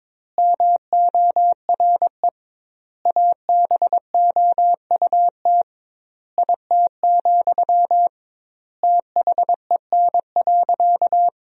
Morse code, heard as a keyed tone, about 22 wpm, 700 Hz. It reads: MORE ABOUT IT, THEN.
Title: 2x Word-Spacing - Sentences from Top 200 Words - 22wpm